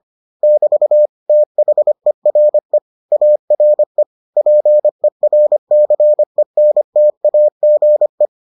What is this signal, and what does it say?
Morse code, heard as a keyed tone, about 25 wpm, 600 Hz. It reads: <BT> THERE ARE PERCENTAGE